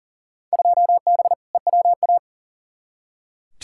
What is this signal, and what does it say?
Morse code, heard as a keyed tone, about 40 wpm, 700 Hz. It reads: 2B EWA
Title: Field Day Exchanges Encore - 40wpm